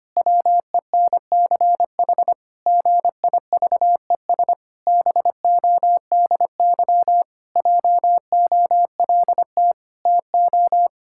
Morse code, heard as a keyed tone, about 25 wpm, 700 Hz. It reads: WENCH GIVES BODY JOLT TO
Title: Pangram Challenge v2 - 25wpm